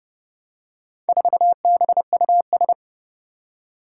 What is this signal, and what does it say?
Morse code, heard as a keyed tone, about 28 wpm, 700 Hz. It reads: 4BUS
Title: Four Challenging Letter-Number Combinations 30wpm